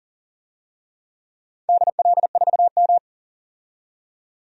Morse code, heard as a keyed tone, about 39 wpm, 700 Hz. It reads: DL4M
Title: DXCC-100 Call Signs - 4 Characters - 40wpm